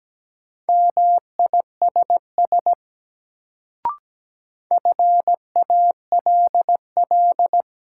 Morse code, heard as a keyed tone, about 17 wpm, 700 Hz. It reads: MISS FALL
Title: Sentences from Top 500 Words - New Word Drill - 17wpm